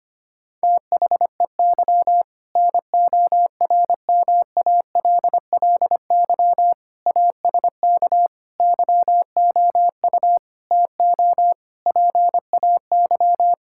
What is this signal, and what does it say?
Morse code, heard as a keyed tone, about 25 wpm, 700 Hz. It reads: THEY NORMALLY ASK YOU TO PAY
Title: Sentences from Top 1000 Words - Review Part 6 - 25wpm